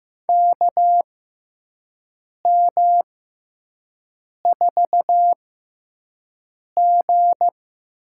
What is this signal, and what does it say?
Morse code, heard as a keyed tone, about 15 wpm, spaced as for 5 wpm, 700 Hz. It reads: KM4G
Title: USA Call Signs - 6 Characters - 6x Character Spacing - 15wpm